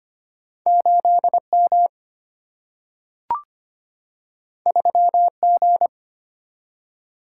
Morse code, heard as a keyed tone, about 25 wpm, 700 Hz. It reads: 8M 3G
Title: Single-Letter Single-Number 25wpm